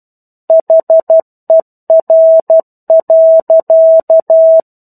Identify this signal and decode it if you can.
Morse code, heard as a keyed tone, about 12 wpm, 650 Hz. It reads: HER.
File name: 01-Chapter-01 12 WPM